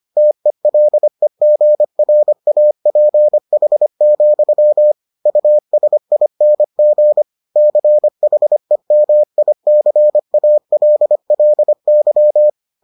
Morse code, heard as a keyed tone, about 25 wpm, 600 Hz. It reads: TELEGRAPH, USING CHEMICALLY